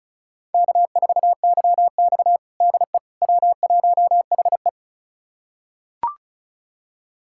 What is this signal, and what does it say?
Morse code, heard as a keyed tone, about 36 wpm, 700 Hz. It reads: K4YX DE W1HE
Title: USA Callsign DE Callsign - 4 Characters - 35wpm